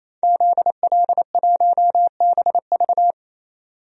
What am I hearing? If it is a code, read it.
Morse code, heard as a keyed tone, about 28 wpm, 700 Hz. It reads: ZL1BV
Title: DXCC-100 Call Signs - 5 Characters - 28wpm